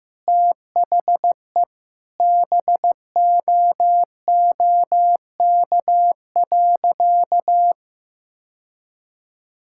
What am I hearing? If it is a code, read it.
Morse code, heard as a keyed tone, about 15 wpm, 700 Hz. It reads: THE BOOK.
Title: Sentences from Top 1000 Words - Review Part 6 - 15wpm